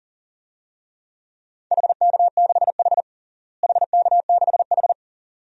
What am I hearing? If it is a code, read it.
Morse code, heard as a keyed tone, about 40 wpm, 700 Hz. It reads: HK6H HK6H
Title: DXCC-100 Call Signs - 4 Characters - Repeated 2x - 40wpm